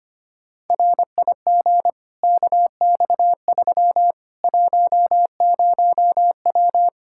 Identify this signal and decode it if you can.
Morse code, heard as a keyed tone, about 25 wpm, 700 Hz. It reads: RIG KX3 10W
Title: Practice QSO - Exchange 2 - Info - 25wpm